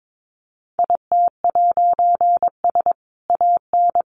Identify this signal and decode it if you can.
Morse code, heard as a keyed tone, about 22 wpm, 700 Hz. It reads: IT'S AN
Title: A Sound Of Thunder 22wpm